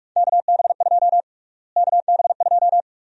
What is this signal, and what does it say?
Morse code, heard as a keyed tone, about 46 wpm, 700 Hz. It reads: KB2 KB2
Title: USA Call Sign Prefixes - Repeated 2x - 45wpm